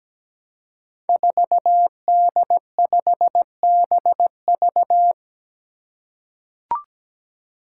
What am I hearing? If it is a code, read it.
Morse code, heard as a keyed tone, about 17 wpm, 700 Hz. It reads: 4D5BV
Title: Five Challenging Letter-Number Combinations 17wpm